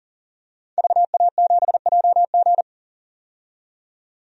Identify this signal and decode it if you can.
Morse code, heard as a keyed tone, about 40 wpm, 700 Hz. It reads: VA7JG
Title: DXCC-100 Call Signs - 5 Characters - 40wpm